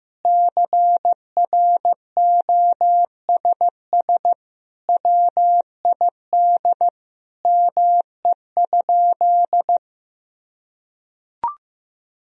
Morse code, heard as a keyed tone, about 15 wpm, 700 Hz. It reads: CROSS WID ME?
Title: Sentences from Top 300 Words and Abbreviations 15wpm